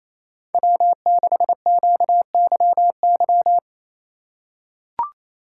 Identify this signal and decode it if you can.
Morse code, heard as a keyed tone, about 27 wpm, 700 Hz. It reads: W6QYY
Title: USA Call Signs - 4x Character Spacing - 28wpm